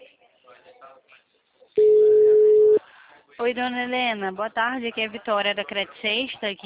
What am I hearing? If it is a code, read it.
Ringback tone (Europe, 425 Hz), with speech.
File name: out- -2014- - - .85184